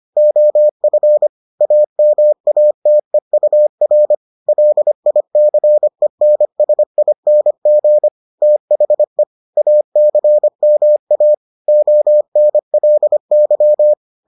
Morse code, heard as a keyed tone, about 25 wpm, 600 Hz. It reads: OF AMATEUR LICENSING THE ACMA ONLY